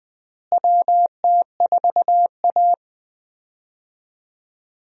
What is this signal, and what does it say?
Morse code, heard as a keyed tone, about 20 wpm, 700 Hz. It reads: WT4A
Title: USA Call Signs - 4 Characters - Repeated 3x - 20wpm